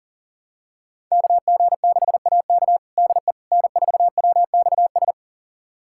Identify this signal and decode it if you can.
Morse code, heard as a keyed tone, about 37 wpm, 700 Hz. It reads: KG6AK DE N4WXS
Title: USA Callsign DE Callsign - 5 Characters - 40wpm